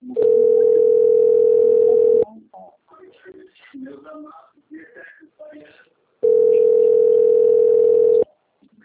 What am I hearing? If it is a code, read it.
Ringback tone (North America), 440 + 480 Hz, with speech.
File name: rg-1600-unknown- - - .20008